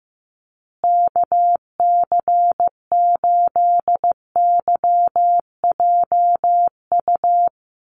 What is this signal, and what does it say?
Morse code, heard as a keyed tone, about 15 wpm, 700 Hz. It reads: KC8YJU
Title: USA Call Signs Repeated 3x 15wpm